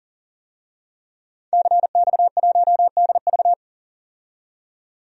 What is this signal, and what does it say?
Morse code, heard as a keyed tone, about 40 wpm, 700 Hz. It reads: CX1DV
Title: DXCC-100 Call Signs - 5 Characters - 40wpm